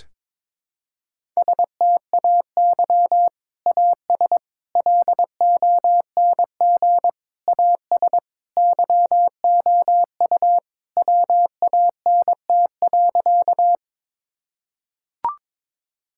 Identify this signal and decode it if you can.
Morse code, heard as a keyed tone, about 22 wpm, 700 Hz. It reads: STAY AS LONG AS YOU WANT.
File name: practice / 22wpm / Sentences with 3-Word Phrases - 22wpm